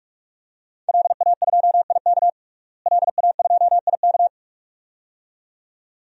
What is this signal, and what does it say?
Morse code, heard as a keyed tone, about 46 wpm, 700 Hz. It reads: LA2IK LA2IK
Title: DXCC-100 Call Signs - 5 Characters - Repeated 2x - 45wpm